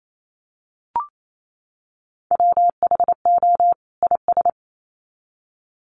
Morse code, heard as a keyed tone, about 28 wpm, 700 Hz. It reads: WHO IS
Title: Sets of 2 Words 28wpm